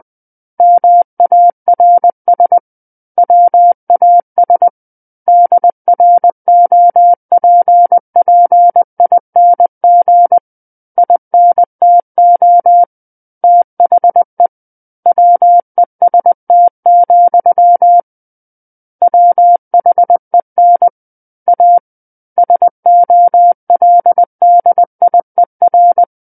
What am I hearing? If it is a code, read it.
Morse code, heard as a keyed tone, about 20 wpm, 700 Hz. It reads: MARS WAS DROPPING INTO THE WEST, WHEN A SOLDIER
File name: 20 WPM